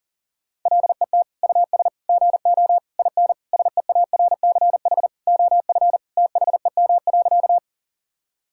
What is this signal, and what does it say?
Morse code, heard as a keyed tone, about 40 wpm, 700 Hz. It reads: LET US GO IN SEARCH OF THEM.
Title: Sentences from Top 1000 Words - Review Part 5 - 40wpm